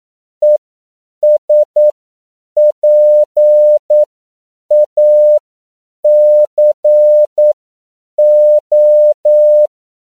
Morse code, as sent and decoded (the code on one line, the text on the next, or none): . ... .--. .- -.-. ---
ESPACO